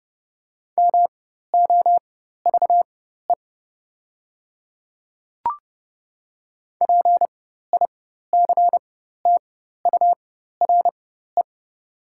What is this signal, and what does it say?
Morse code, heard as a keyed tone, about 30 wpm, spaced as for 14 wpm, 700 Hz. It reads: MOVE PICTURE
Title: 4x Character Spacing - Top 200 Words - 30wpm